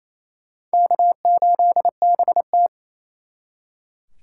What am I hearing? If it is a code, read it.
Morse code, heard as a keyed tone, about 36 wpm, 700 Hz. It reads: K8BT